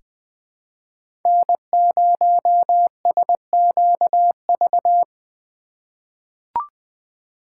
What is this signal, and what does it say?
Morse code, heard as a keyed tone, about 20 wpm, 700 Hz. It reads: N0SQV
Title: USA Call Signs - 5 Characters - 4x Character Spacing - 20wpm